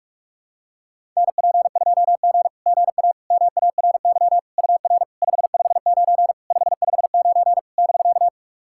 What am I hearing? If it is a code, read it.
Morse code, heard as a keyed tone, about 46 wpm, 660 Hz. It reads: NP2G GA MARY UR 559 559 <BK>
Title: SOTA QSO Practice 45wpm